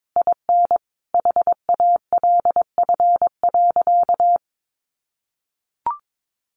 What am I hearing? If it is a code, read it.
Morse code, heard as a keyed tone, about 22 wpm, 700 Hz. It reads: IN HALF.
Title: Sentences from Top 1100 Words 22wpm